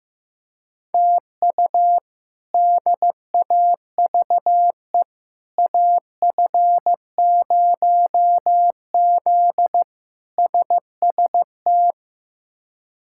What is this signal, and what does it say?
Morse code, heard as a keyed tone, about 15 wpm, 700 Hz. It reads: TU DAVE AF0Z SST
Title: K1USN SST QSO Practice - 15wpm